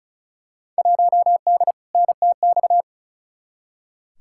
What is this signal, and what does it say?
Morse code, heard as a keyed tone, about 36 wpm, 700 Hz. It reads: 1D NTX
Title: Field Day Exchanges - 35wpm